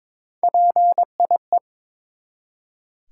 Morse code, heard as a keyed tone, about 21 wpm, 700 Hz. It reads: PIE
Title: Three Letter Words - 22wpm